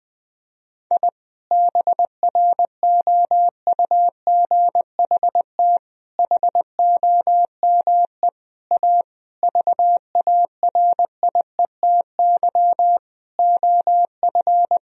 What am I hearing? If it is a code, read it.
Morse code, heard as a keyed tone, about 20 wpm, 700 Hz. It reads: I BROUGHT HOME A VARIETY OF
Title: Sentences from Top 1000 Words - Review Part 4 - 20wpm